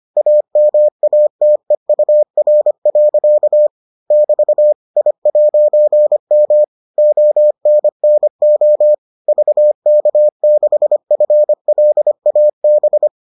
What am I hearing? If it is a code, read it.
Morse code, heard as a keyed tone, about 25 wpm, 600 Hz. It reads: AMATEUR. <BT> I'M ONNO VK6FLAB